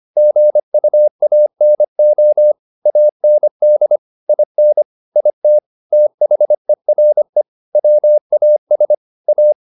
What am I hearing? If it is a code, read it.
Morse code, heard as a keyed tone, about 25 wpm, 600 Hz. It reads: GUANO AND IN IT THERE WAS A
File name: the old man and the sea part 1 25wpm